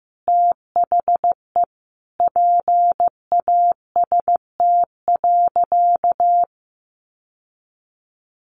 Morse code, heard as a keyed tone, about 15 wpm, 700 Hz. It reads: THE PAST.
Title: Sentences from Top 1200 Words 15wpm